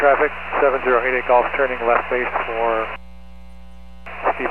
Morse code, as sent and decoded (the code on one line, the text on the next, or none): none